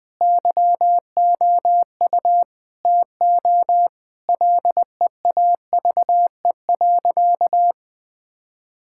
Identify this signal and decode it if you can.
Morse code, heard as a keyed tone, about 20 wpm, 700 Hz. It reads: YOU TO LEAVE.
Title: Sentences with 5-Word Phrases - 20wpm